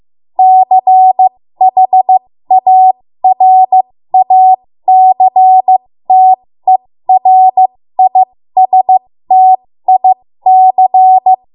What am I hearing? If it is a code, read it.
Morse code, heard as a keyed tone, about 15 wpm, spaced as for 13 wpm, 750 Hz. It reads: CHARACTERISTIC
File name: Morse / 13WPM